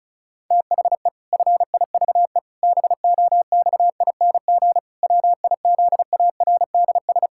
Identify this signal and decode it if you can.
Morse code, heard as a keyed tone, about 34 wpm, 700 Hz. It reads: THE FIVE BOXING WIZARDS
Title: Pangram Challenge v2 - 35wpm